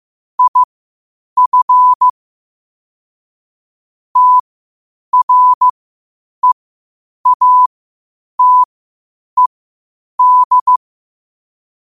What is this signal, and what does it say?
Morse code, heard as a keyed tone, about 15 wpm, spaced as for 8 wpm, 1000 Hz. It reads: IF TREATED
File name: qotdfiles / 8wpm